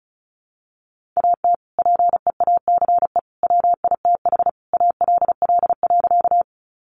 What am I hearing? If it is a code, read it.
Morse code, heard as a keyed tone, about 34 wpm, 700 Hz. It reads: AT PEACE WITH ALL.
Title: Sentences from Top 1500 Words 35wpm